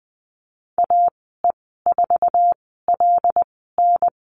Morse code, heard as a keyed tone, about 20 wpm, spaced as for 14 wpm, 700 Hz. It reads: AE4LN
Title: 088 Lesson 15g - Number 5 - Call Signs 20wpm